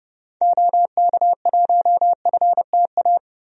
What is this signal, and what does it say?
Morse code, heard as a keyed tone, about 30 wpm, 700 Hz. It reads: OK1FTA
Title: DXCC-100 Call Signs - 6 Characters - 30wpm